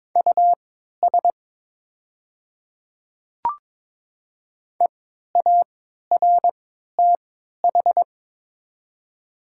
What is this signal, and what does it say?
Morse code, heard as a keyed tone, about 22 wpm, spaced as for 12 wpm, 690 Hz. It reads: US EARTH
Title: 3x Character Spacing - Top 300 Words - 22wpm